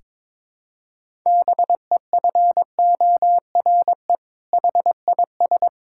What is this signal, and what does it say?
Morse code, heard as a keyed tone, about 22 wpm, 700 Hz. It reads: BEFORE HIS
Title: Sentences from Top 1000 Words - Review Part 5 - 22wpm